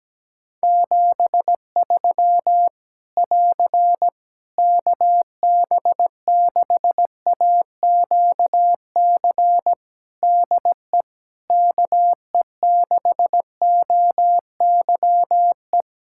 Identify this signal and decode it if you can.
Morse code, heard as a keyed tone, about 17 wpm, 700 Hz. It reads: 73 <AR> KB6AQC DE KE6OYE
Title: Practice QSO - Exchange 4 - Ending - 17wpm